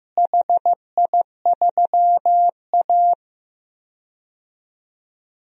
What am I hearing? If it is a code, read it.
Morse code, heard as a keyed tone, about 16 wpm, 700 Hz. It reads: HI3A
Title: DXCC-100 Call Signs - 4 Characters - Repeated 2x - 15wpm